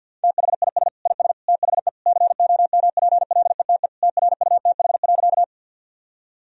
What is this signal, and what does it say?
Morse code, heard as a keyed tone, about 50 wpm, 700 Hz. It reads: THIS IS THE COMPLETE TRUTH.
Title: Sentences from Top 1000 Words - Review Part 2 - 50wpm